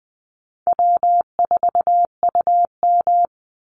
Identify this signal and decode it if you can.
Morse code, heard as a keyed tone, about 20 wpm, 700 Hz. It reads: W4UM
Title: Easy Call Signs Repeated 3x - 20wpm